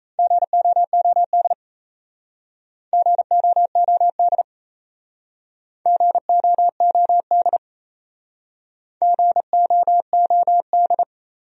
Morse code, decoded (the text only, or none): GOOD GOOD GOOD GOOD